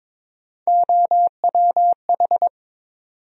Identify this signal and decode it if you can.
Morse code, heard as a keyed tone, about 22 wpm, 700 Hz. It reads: OWH
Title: USA Call Sign Postfixes - 3 Letters 22wpm